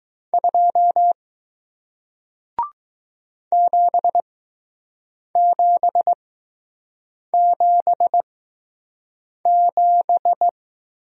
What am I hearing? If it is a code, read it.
Morse code, heard as a keyed tone, about 20 wpm, spaced as for 6 wpm, 700 Hz. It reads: 2 7777